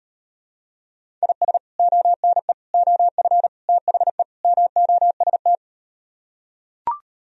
Morse code, heard as a keyed tone, about 38 wpm, 700 Hz. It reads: IS ONE OF THE MOST